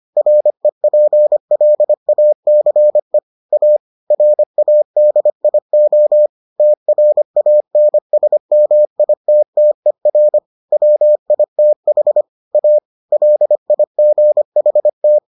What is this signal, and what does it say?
Morse code, heard as a keyed tone, about 25 wpm, 600 Hz. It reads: REPLACE A RADIO TRANSMITTER WITH A LIGHT